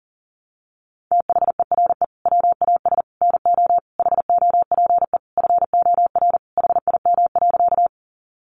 Morse code, heard as a keyed tone, about 40 wpm, 700 Hz. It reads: THERE WAS NO HOPE FOR HIM.
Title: Sentences from Top 1000 Words 40wpm